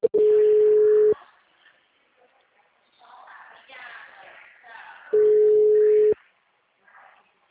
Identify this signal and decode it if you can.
Ringback tone (Europe, 425 Hz).